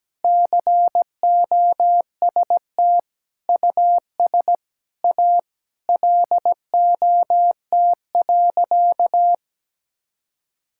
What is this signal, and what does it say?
Morse code, heard as a keyed tone, about 17 wpm, 700 Hz. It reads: COST US A LOT.
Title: Sentences from Top 1000 Words - Review Part 5 - 17wpm